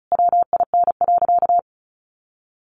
Morse code, heard as a keyed tone, about 34 wpm, 700 Hz. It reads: WIN.
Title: Sentences from Top 1500 Words 35wpm